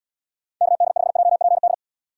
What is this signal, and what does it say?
Morse code, heard as a keyed tone, about 78 wpm, 700 Hz. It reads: KD5JPD